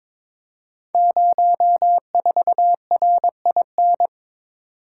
Morse code, heard as a keyed tone, about 22 wpm, 700 Hz. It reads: 04RIN
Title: Code Groups - 5 Letter:Numbers - 22wpm